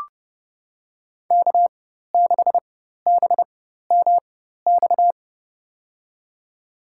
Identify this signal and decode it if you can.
Morse code, heard as a keyed tone, about 30 wpm, spaced as for 14 wpm, 700 Hz. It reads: K6BMX
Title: USA Call Signs - 4x Character Spacing - 30wpm